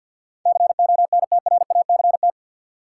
Morse code, heard as a keyed tone, about 53 wpm, 700 Hz. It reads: CONTRACT
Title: Sentences from Top 1400 Words - New Word Drill Alt - 50wpm